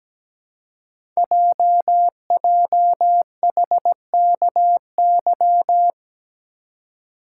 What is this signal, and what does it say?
Morse code, heard as a keyed tone, about 17 wpm, 700 Hz. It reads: JJHKY
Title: Code Groups - 5 Letters - 17wpm